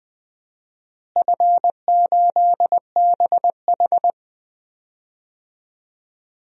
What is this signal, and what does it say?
Morse code, heard as a keyed tone, about 20 wpm, 700 Hz. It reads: F8BH
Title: DXCC-100 Call Signs - 4 Characters - Repeated 3x - 20wpm